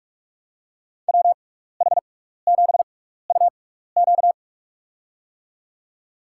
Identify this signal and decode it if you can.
Morse code, heard as a keyed tone, about 46 wpm, spaced as for 16 wpm, 700 Hz. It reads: WH7UQ
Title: USA Call Signs - 6x Character Spacing - 45wpm